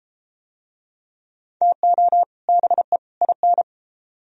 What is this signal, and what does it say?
Morse code, heard as a keyed tone, about 33 wpm, 700 Hz. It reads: TO BE IN